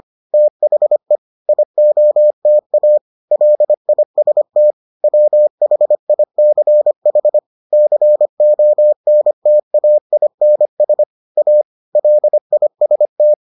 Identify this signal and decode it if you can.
Morse code, heard as a keyed tone, about 25 wpm, 600 Hz. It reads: THE IOTA LIST WHICH CONTAINS A LIST